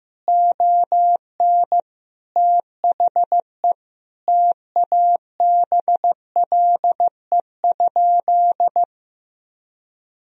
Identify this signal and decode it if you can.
Morse code, heard as a keyed tone, about 15 wpm, 700 Hz. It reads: ON THE TABLE?
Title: Sentences from Top 1000 Words - Review Part 4 - 15wpm